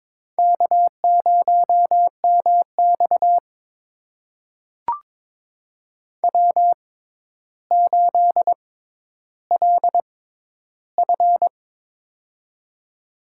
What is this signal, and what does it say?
Morse code, heard as a keyed tone, about 22 wpm, spaced as for 11 wpm, 700 Hz. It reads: K0MX W8LF
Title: USA Call Signs - 4 Characters - 6x Character Spacing - 22wpm